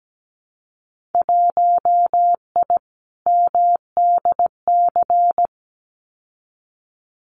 Morse code, heard as a keyed tone, about 17 wpm, 700 Hz. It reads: 1I MDC
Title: Winter Field Day Exchanges 17wpm